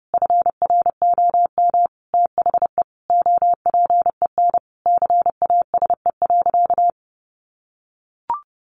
Morse code, heard as a keyed tone, about 30 wpm, 700 Hz. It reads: FROM THE OPEN CASE.
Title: Sentences from Top 1000 Words 30wpm